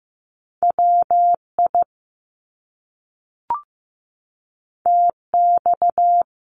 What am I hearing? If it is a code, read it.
Morse code, heard as a keyed tone, about 15 wpm, 700 Hz. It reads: WI TX
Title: US States 15wpm